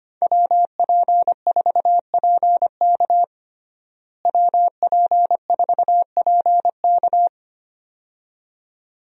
Morse code, heard as a keyed tone, about 25 wpm, 700 Hz. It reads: WP4PK WP4PK
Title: USA Call Signs - 5 Characters - Repeated 2x - 25wpm